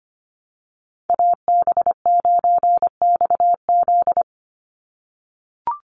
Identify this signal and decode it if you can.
Morse code, heard as a keyed tone, about 25 wpm, 700 Hz. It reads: AB9XZ
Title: USA Call Signs 25wpm